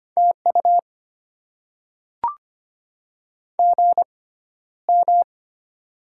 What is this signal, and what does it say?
Morse code, heard as a keyed tone, about 25 wpm, spaced as for 13 wpm, 700 Hz. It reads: TU GM
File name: Top 20 QSO Elements - 6x Character Spacing - 25wpm